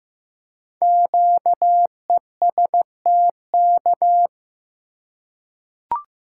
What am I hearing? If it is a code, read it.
Morse code, heard as a keyed tone, about 15 wpm, 700 Hz. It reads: QESTK
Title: Code Groups - 5 Letters - 15wpm